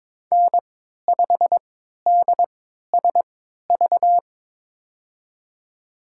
Morse code, heard as a keyed tone, about 22 wpm, spaced as for 13 wpm, 700 Hz. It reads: N5DSV
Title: USA Call Signs - 5 Characters - 3x Character Spacing - 22wpm